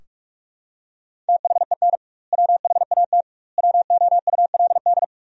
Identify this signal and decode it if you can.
Morse code, heard as a keyed tone, about 41 wpm, 700 Hz. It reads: THEN WHAT WOULD
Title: 2x Word-Spacing - Sentences from Top 400 Words - 45wpm